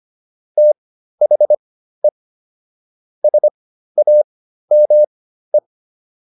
Morse code, heard as a keyed tone, about 25 wpm, spaced as for 13 wpm, 600 Hz. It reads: THE SAME